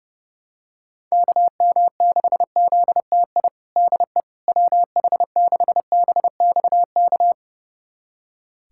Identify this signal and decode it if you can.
Morse code, heard as a keyed tone, about 30 wpm, 700 Hz. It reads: KM6ZTI DE WH6BXK
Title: USA Callsign DE Callsign - 6 Characters - 30wpm